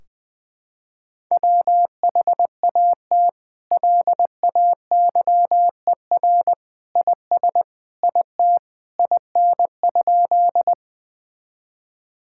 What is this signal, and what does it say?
Morse code, heard as a keyed tone, about 20 wpm, 700 Hz. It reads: WHAT LAYER IS IT IN?
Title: Sentences from Top 1600 Words 20wpm